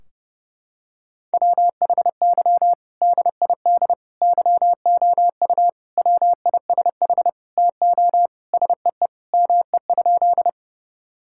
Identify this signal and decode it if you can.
Morse code, heard as a keyed tone, about 30 wpm, 700 Hz. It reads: WHY DID YOU WISH TO SEE ME?
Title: Sentences from Top 1600 Words 30wpm